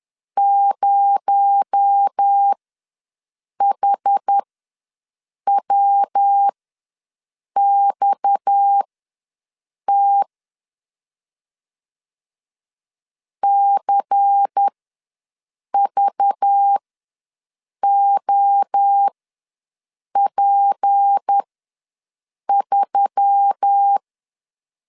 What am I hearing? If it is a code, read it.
Morse code, heard as a keyed tone, about 10 wpm, spaced as for 6 wpm, 800 Hz. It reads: 0HWXT CVOP3
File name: LEZIONI / 6 PKX0 / 6 PKX0 ws 10wpm 4220